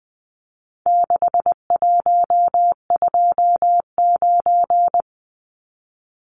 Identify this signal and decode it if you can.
Morse code, heard as a keyed tone, about 20 wpm, 700 Hz. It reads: 6129
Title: Numbers - 4-Digit 20wpm